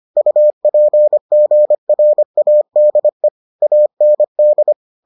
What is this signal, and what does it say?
Morse code, heard as a keyed tone, about 25 wpm, 600 Hz. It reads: UPGRADE AND